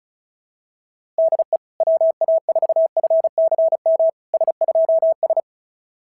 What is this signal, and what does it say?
Morse code, heard as a keyed tone, about 36 wpm, 690 Hz. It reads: DE WA4FCM S2S
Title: SOTA QSO Practice 35wpm